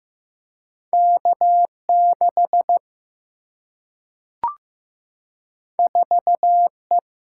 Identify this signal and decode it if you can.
Morse code, heard as a keyed tone, about 15 wpm, 700 Hz. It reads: K6 4E
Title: Single-Letter Single-Number 15wpm